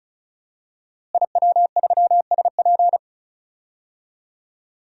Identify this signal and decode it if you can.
Morse code, heard as a keyed tone, about 36 wpm, 700 Hz. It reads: IW3SP
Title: DXCC-100 Call Signs - 5 Characters - Repeated 2x - 35wpm